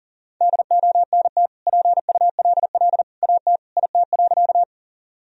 Morse code, heard as a keyed tone, about 40 wpm, 700 Hz. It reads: DONT PULL AT IT.
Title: Sentences from Top 1000 Words - Review Part 2 - 40wpm